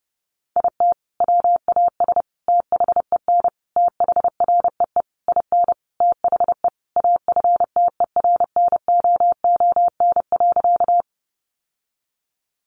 Morse code, heard as a keyed tone, about 30 wpm, 700 Hz. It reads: IT WAS THEN THREE IN THE AFTERNOON.
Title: Sentences from Top 1500 Words 30wpm